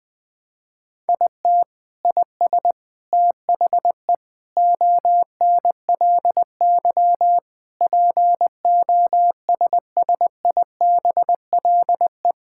Morse code, heard as a keyed tone, about 20 wpm, 700 Hz. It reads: IT IS THE ONLY POSSIBLE
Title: Sentences from Top 1600 Words 20wpm